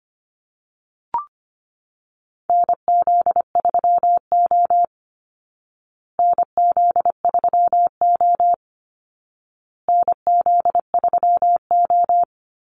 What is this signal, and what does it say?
Morse code, heard as a keyed tone, about 25 wpm, 700 Hz. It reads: NZ3O NZ3O NZ3O
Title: Call Signs Repeated 2x - 25wpm